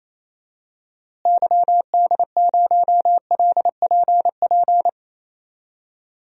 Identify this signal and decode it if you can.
Morse code, heard as a keyed tone, about 28 wpm, 700 Hz. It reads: YD0LPP
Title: DXCC-100 Call Signs - 6 Characters - Repeated 2x - 28wpm